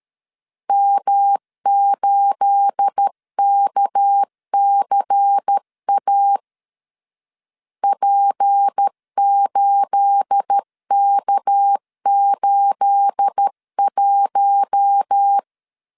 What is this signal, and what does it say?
Morse code, heard as a keyed tone, about 13 wpm, 800 Hz. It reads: M8KCA P8K81
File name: LEZIONI / 9 28 12wpm 2351